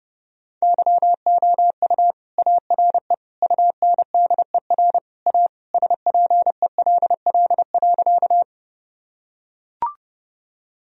Morse code, heard as a keyed tone, about 30 wpm, 700 Hz. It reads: YOU ARE UNDER A SPELL.
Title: Sentences from Top 1000 Words - Review Part 1 - 30wpm